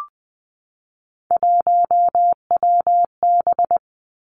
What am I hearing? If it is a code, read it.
Morse code, heard as a keyed tone, about 19 wpm, 700 Hz. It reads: 1WB
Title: Three Challenging Letter-Number Combinations 20wpm